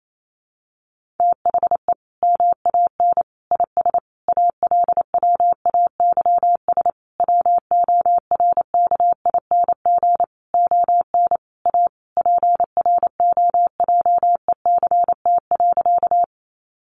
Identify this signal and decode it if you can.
Morse code, heard as a keyed tone, about 28 wpm, 700 Hz. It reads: THE MAN IS ALWAYS WORKING ON A PROJECT.
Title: Sentences from Top 1400 Words 28wpm